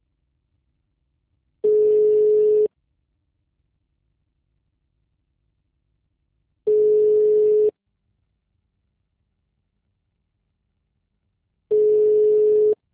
Ringback tone (Europe, 425 Hz).